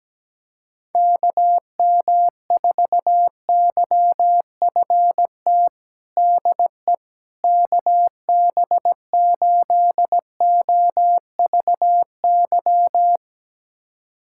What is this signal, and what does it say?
Morse code, heard as a keyed tone, about 17 wpm, 700 Hz. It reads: KM4YFT DE KB8OVY
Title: USA Callsign DE Callsign - 6 Characters - 17wpm